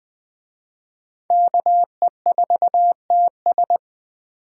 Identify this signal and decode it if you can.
Morse code, heard as a keyed tone, about 20 wpm, 700 Hz. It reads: KE4TS
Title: Speed Racing - USA 2x2 Call Signs 17wpm